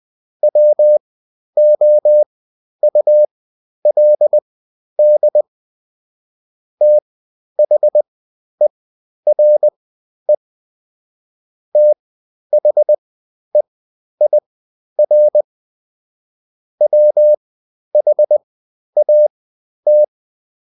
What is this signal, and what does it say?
Morse code, heard as a keyed tone, about 20 wpm, spaced as for 11 wpm, 600 Hz. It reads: WOULD THERE THEIR WHAT
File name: Top-100-Words-10